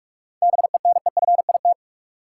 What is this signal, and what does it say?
Morse code, heard as a keyed tone, about 42 wpm, 700 Hz. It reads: BENEFIT
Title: Sentences from Top 1000 Words - Review Part 6 - New Word Drill - 45wpm